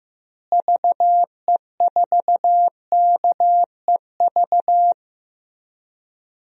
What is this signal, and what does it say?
Morse code, heard as a keyed tone, about 15 wpm, 700 Hz. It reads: VE4KEV
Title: DXCC-100 Call Signs - 6 Characters - Repeated 3x - 15wpm